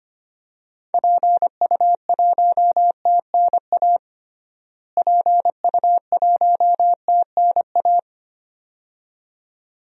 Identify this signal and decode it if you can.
Morse code, heard as a keyed tone, about 25 wpm, 700 Hz. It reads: PU1TNA PU1TNA
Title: DXCC-100 Call Signs - 6 Characters - Repeated 2x - 25wpm